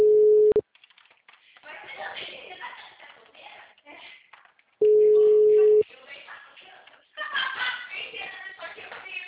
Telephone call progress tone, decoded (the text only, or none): ringback tone